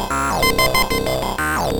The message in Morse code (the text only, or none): none